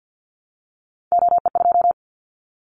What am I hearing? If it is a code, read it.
Morse code, heard as a keyed tone, about 50 wpm, 700 Hz. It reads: OE2
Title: Callsign Prefixes for Top 25 DXCC - 50wpm